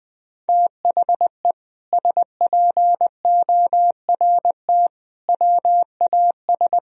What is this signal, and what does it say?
Morse code, heard as a keyed tone, about 20 wpm, 700 Hz. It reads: THE SPORT WAS
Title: Sentences from Top 1000 Words - Review Part 4 - 20wpm